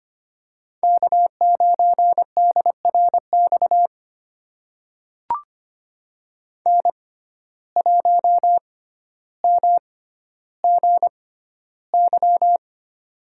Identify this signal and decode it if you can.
Morse code, heard as a keyed tone, about 25 wpm, spaced as for 13 wpm, 700 Hz. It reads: K9DRX N1MGY
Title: USA Call Signs - 5 Characters - 6x Character Spacing - 25wpm